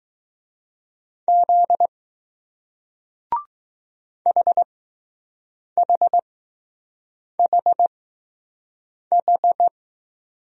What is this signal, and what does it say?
Morse code, heard as a keyed tone, about 21 wpm, spaced as for 6 wpm, 700 Hz. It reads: Z HHHH